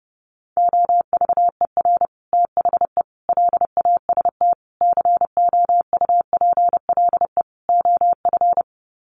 Morse code, heard as a keyed tone, about 30 wpm, 700 Hz. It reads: OVER THE LAST COUPLE OF
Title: Sets of 5 Words Encore 30wpm